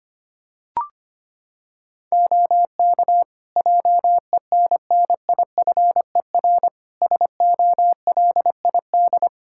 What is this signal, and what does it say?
Morse code, heard as a keyed tone, about 25 wpm, 700 Hz. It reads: OK JENNIFER SOLID
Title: Practice QSO - Exchange 3 - More Info - 25wpm